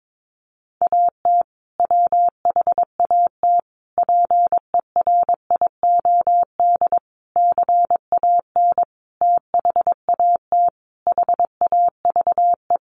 Morse code, heard as a keyed tone, about 22 wpm, 700 Hz. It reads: AT WHAT PERIOD CAN THAT HAVE
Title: Sentences from Top 800 Words 22wpm